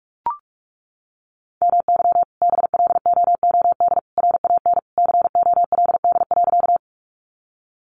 Morse code, heard as a keyed tone, about 44 wpm, 700 Hz. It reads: MY BLOOD RAN COLD.
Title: Sentences from Top 1100 Words 45wpm